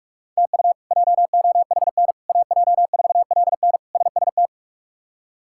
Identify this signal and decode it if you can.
Morse code, heard as a keyed tone, about 46 wpm, 700 Hz. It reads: TU JOHN AJ4LN SST
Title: K1USN SST QSO Practice - 45wpm